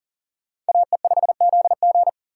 Morse code, heard as a keyed tone, about 53 wpm, 700 Hz. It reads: AE5ZG